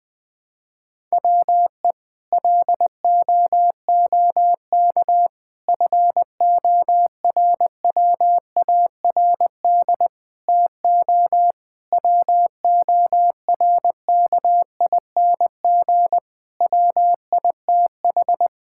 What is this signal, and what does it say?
Morse code, heard as a keyed tone, about 21 wpm, 700 Hz. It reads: WE LOOK FORWARD TO WORKING WITH
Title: Sentences with 6-Word Phrases - Spoken Beginning - 20wpm